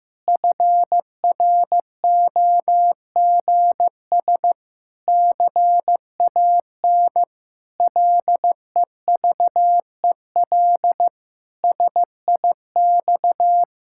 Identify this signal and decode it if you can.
Morse code, heard as a keyed tone, about 16 wpm, 700 Hz. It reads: FROGS CAN LEVEL SIX
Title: Pangram Challenge v2 - 15wpm